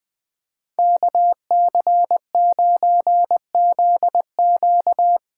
Morse code, heard as a keyed tone, about 21 wpm, 700 Hz. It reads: KC9ZQ
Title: USA Callsign DE Callsign - 5 Characters - 20wpm